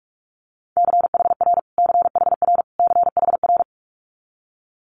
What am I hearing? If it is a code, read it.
Morse code, heard as a keyed tone, about 46 wpm, 700 Hz. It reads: CHR CHR CHR
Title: Multi-letter Phonemes Repeated 3x 45wpm